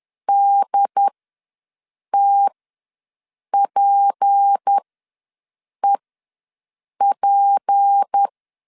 Morse code, heard as a keyed tone, about 10 wpm, spaced as for 6 wpm, 800 Hz. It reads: DTPEP
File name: LEZIONI / 6 PKX0 / 6 P ws 10wpm 3751